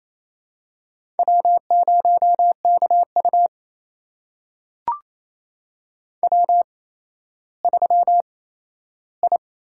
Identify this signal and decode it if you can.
Morse code, heard as a keyed tone, about 28 wpm, spaced as for 14 wpm, 700 Hz. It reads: W0KU W3I
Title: USA Call Signs - 4 Characters - 8x Character Spacing - 28wpm